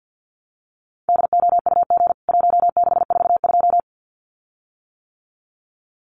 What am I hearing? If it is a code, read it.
Morse code, heard as a keyed tone, about 52 wpm, 700 Hz. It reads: DOUG 1642
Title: CWT Exchange 50wpm